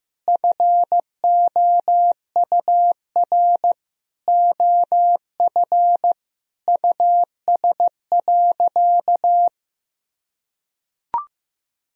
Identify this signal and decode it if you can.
Morse code, heard as a keyed tone, about 15 wpm, 700 Hz. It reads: FOUR OF US.
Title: Sentences from Top 1000 Words - Review Part 1 - 15wpm